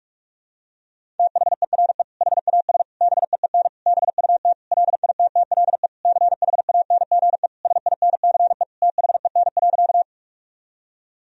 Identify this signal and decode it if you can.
Morse code, heard as a keyed tone, about 43 wpm, 700 Hz. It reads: THERE HAS BEEN BUT LITTLE CHANGE SINCE THEN.
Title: Sentences from Top 1000 Words - Review Part 1 - 45wpm